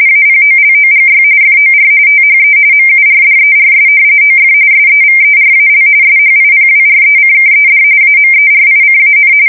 RTTY, about 45.45 baud. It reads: APOLOGY FOR THE MURDERS OF SEVERAL SHIPWRECKED AMERICAN
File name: fsk170 / usnavy3